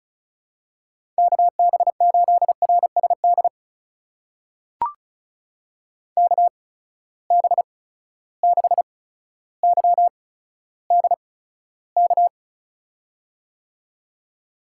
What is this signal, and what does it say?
Morse code, heard as a keyed tone, about 34 wpm, spaced as for 15 wpm, 700 Hz. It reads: KB8RSD KB6YDK
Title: USA Call Signs - 6 Characters - 8x Character Spacing - 35wpm